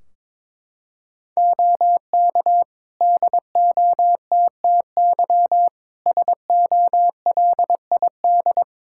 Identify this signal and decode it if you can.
Morse code, heard as a keyed tone, about 22 wpm, 700 Hz. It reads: OK DOTTY SOLID
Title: K1USN SST QSO Practice 22wpm